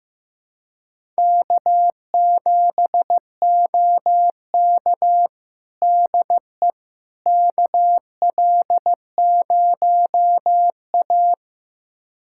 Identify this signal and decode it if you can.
Morse code, heard as a keyed tone, about 16 wpm, 700 Hz. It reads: K7OK DE KL0A
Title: USA Callsign DE Callsign - 4 Characters - 15wpm